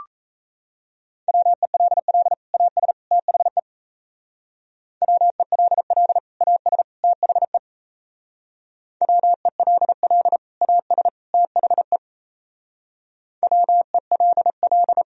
Morse code, heard as a keyed tone, about 34 wpm, 700 Hz. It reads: WELL AS THE WELL AS THE WELL AS THE WELL
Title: Speed-Racing - Sets of 3 Words 28wpm